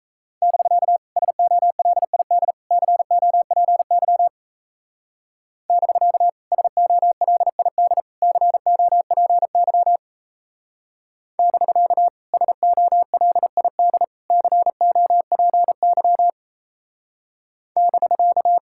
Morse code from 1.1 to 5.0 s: SOLID COPY

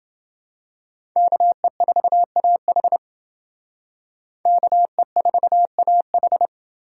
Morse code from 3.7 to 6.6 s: KE4AH

